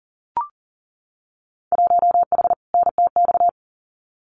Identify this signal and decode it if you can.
Morse code, heard as a keyed tone, about 38 wpm, 700 Hz. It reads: 1H NTX
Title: Winter Field Day Exchanges 40wpm